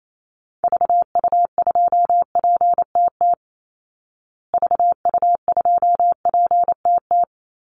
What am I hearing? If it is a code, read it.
Morse code, heard as a keyed tone, about 28 wpm, 700 Hz. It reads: VU2PTT VU2PTT
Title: Call Signs Repeated 3x - 28wpm